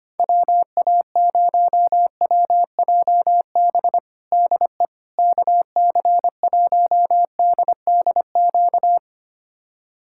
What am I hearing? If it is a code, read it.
Morse code, heard as a keyed tone, about 25 wpm, 700 Hz. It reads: WA0WJB DE KC1DDQ
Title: USA Callsign DE Callsign - 6 Characters - 25wpm